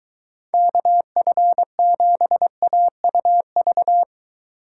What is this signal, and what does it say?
Morse code, heard as a keyed tone, about 23 wpm, 700 Hz. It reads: KF7AUV